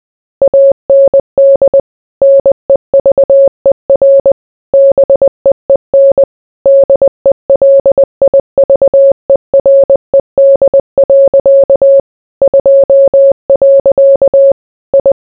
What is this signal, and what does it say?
Morse code, heard as a keyed tone, about 20 wpm, 550 Hz. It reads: AND NEVER BEEN DELIVERED. 2. I